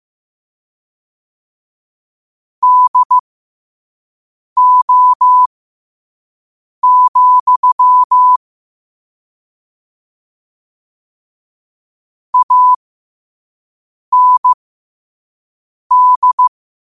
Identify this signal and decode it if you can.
Morse code, heard as a keyed tone, about 15 wpm, spaced as for 5 wpm, 1000 Hz. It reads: DO, AND
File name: 2024-10-07-5wpm